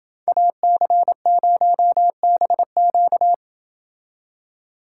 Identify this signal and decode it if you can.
Morse code, heard as a keyed tone, about 28 wpm, 700 Hz. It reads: AC0BQ